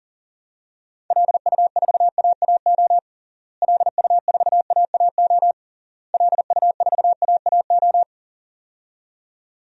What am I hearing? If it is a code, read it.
Morse code, heard as a keyed tone, about 40 wpm, 700 Hz. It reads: LU4AAO LU4AAO LU4AAO
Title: DXCC-100 Call Signs - 6 Characters - Repeated 3x - 40wpm